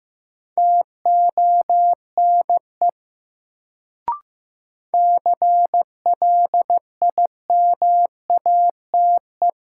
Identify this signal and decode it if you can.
Morse code, heard as a keyed tone, about 15 wpm, 700 Hz. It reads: TONE CLIMATE
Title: Sentences from Top 1600 Words - New Word Drill - 15wpm